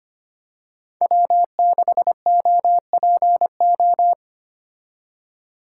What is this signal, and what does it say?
Morse code, heard as a keyed tone, about 25 wpm, 700 Hz. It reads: W6OPO
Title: K1USN SST QSO Practice - 25wpm